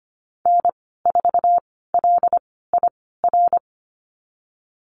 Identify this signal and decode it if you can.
Morse code, heard as a keyed tone, about 25 wpm, spaced as for 16 wpm, 700 Hz. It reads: N4LIR
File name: Lesson 15g - Number 5 - Call Signs 25wpm